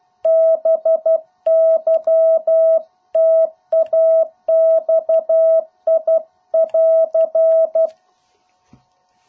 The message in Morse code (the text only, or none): BY TAXI <AR>